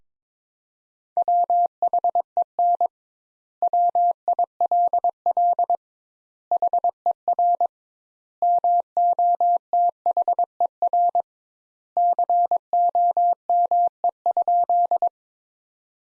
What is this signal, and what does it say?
Morse code, heard as a keyed tone, about 22 wpm, 700 Hz. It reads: WHEN WILL HER MOTHER COME?
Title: Sentences Top 200 Words - 22wpm